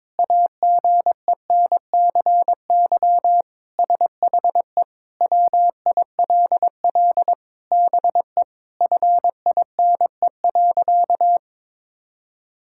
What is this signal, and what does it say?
Morse code, heard as a keyed tone, about 21 wpm, 700 Hz. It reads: AGENCY SHE WILL BE FINE.
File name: Sentences with 7-Word Phrases - 22wpm